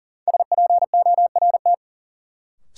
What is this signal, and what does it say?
Morse code, heard as a keyed tone, about 40 wpm, 700 Hz. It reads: SPORT
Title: Sentences from Top 1000 Words - Review Part 4 - New Word Drill - 40wpm